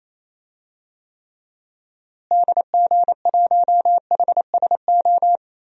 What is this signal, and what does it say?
Morse code, heard as a keyed tone, about 28 wpm, 700 Hz. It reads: DG1HSO
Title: DXCC-100 Call Signs - 6 Characters - Repeated 2x - 28wpm